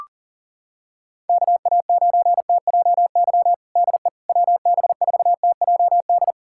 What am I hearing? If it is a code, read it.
Morse code, heard as a keyed tone, about 40 wpm, 700 Hz. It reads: KA9TJY DE WB4TJD
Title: USA Callsign DE Callsign 40wpm